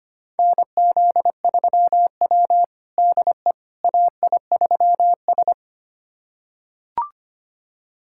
Morse code, heard as a keyed tone, about 25 wpm, 700 Hz. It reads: NZ3W DE AI3S
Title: USA Callsign DE Callsign - 4 Characters - 25wpm